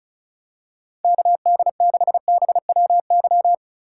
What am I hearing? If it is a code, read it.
Morse code, heard as a keyed tone, about 36 wpm, 700 Hz. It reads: KD6BWY
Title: USA Call Signs - 3x Character Spacing - 35wpm